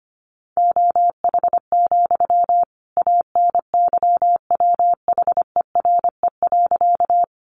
Morse code, heard as a keyed tone, about 25 wpm, 700 Hz. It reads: OH, ANYWHERE.
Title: Sentences from Top 1400 Words 25wpm